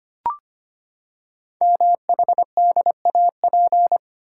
Morse code, heard as a keyed tone, about 25 wpm, 700 Hz. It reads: MHDAP
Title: Code Groups - 5 Letters - 25wpm